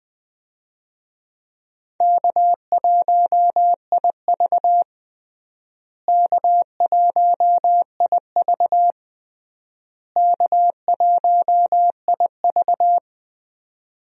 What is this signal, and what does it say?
Morse code, heard as a keyed tone, about 20 wpm, 700 Hz. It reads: K1IV K1IV K1IV
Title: USA Call Signs - 4 Characters - Repeated 3x - 20wpm